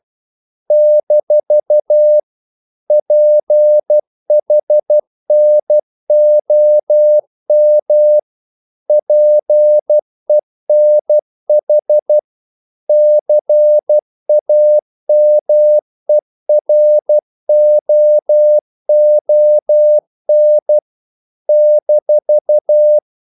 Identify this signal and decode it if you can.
Morse code, heard as a keyed tone, about 12 wpm, 600 Hz. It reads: - PHNOM PENH CAMEROON -
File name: CW / Practice / NationalCapitals12